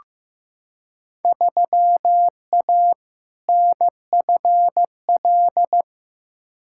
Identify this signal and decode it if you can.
Morse code, heard as a keyed tone, about 15 wpm, 700 Hz. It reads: 3A NFL
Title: Field Day Exchanges - 15wpm